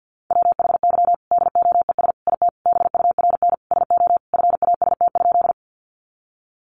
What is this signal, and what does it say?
Morse code, heard as a keyed tone, about 50 wpm, 700 Hz. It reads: WHY DOES IT BURN SO FAST?
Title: Sentences from Top 1300 Words 50wpm